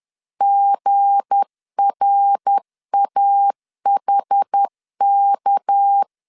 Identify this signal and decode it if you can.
Morse code, heard as a keyed tone, about 10 wpm, 800 Hz. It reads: GRAHK
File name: LEZIONI / 8 YJ45 / 8 YJ4 10wpm 2633